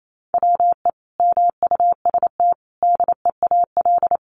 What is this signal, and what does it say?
Morse code, heard as a keyed tone, about 28 wpm, 700 Hz. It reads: WE MUST DEAL
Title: Sentences from Top 1400 Words 28wpm